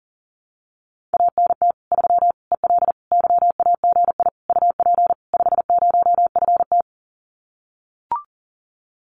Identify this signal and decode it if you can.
Morse code, heard as a keyed tone, about 40 wpm, 700 Hz. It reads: ANT 3 EL YAGI UP 50FT
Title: Common QSO Phrases Encore 40wpm